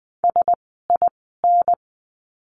Morse code, heard as a keyed tone, about 20 wpm, spaced as for 15 wpm, 700 Hz. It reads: SIN
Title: 054 Lesson 10f - Letter r - Words 20wpm